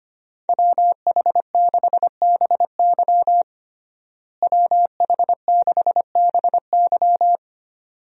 Morse code, heard as a keyed tone, about 25 wpm, 700 Hz. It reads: WH6BY WH6BY
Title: USA Call Signs - 5 Characters - Repeated 2x - 25wpm